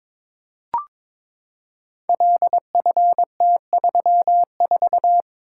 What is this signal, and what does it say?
Morse code, heard as a keyed tone, about 22 wpm, 700 Hz. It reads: LFT34
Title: Code Groups - 5 Letter:Numbers - 22wpm